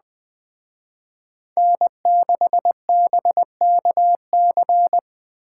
Morse code, heard as a keyed tone, about 19 wpm, 700 Hz. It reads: N6BKC